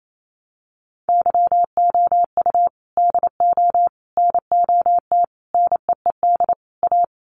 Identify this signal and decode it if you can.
Morse code, heard as a keyed tone, about 28 wpm, 690 Hz. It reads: YOU DO NOT NEED A
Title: Sets of 5 Words Encore 28wpm